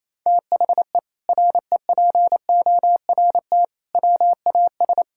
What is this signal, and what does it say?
Morse code, heard as a keyed tone, about 28 wpm, 700 Hz. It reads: THE REPORT WAS
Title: Sentences from Top 1000 Words - Review Part 6 - 28wpm